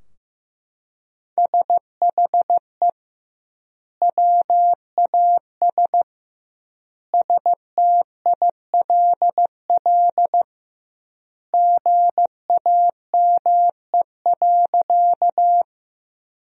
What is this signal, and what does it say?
Morse code, heard as a keyed tone, about 15 wpm, 700 Hz. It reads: SHE WAS STILL GAME.
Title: 2x Word-Spacing - Sentences from Top 500 Words - 15wpm